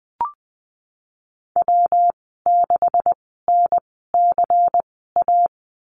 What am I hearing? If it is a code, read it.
Morse code, heard as a keyed tone, about 20 wpm, spaced as for 15 wpm, 700 Hz. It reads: W6NCA
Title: 147 Lesson 24g - Letter y - Call Signs 20wpm